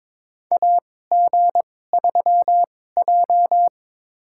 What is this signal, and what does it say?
Morse code, heard as a keyed tone, about 22 wpm, spaced as for 16 wpm, 700 Hz. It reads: AG3J
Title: USA Call Signs - 4 Characters - 2x Character Spacing - 22wpm